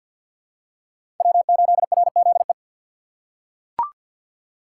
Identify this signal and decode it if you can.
Morse code, heard as a keyed tone, about 47 wpm, 690 Hz. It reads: W8RZE
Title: USA Call Signs - 5 Characters - 6x Character Spacing - 50wpm